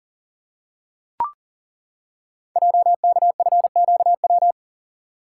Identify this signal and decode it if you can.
Morse code, heard as a keyed tone, about 41 wpm, 690 Hz. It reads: JKFQW
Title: Code Groups - 5 Letters - 40wpm